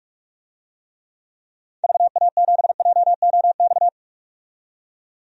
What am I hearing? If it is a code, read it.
Morse code, heard as a keyed tone, about 42 wpm, 700 Hz. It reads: VA7JOX
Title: DXCC-100 Call Signs - 6 Characters - 45wpm